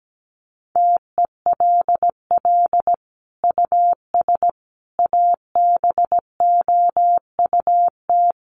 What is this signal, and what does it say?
Morse code, heard as a keyed tone, about 17 wpm, 700 Hz. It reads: TELL US ABOUT
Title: Sentences from Top 1200 Words 17wpm